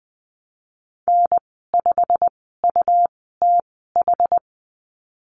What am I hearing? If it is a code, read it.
Morse code, heard as a keyed tone, about 20 wpm, spaced as for 14 wpm, 700 Hz. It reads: N5UTH
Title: 161 Lesson 26g - Letter g - Call Signs 20wpm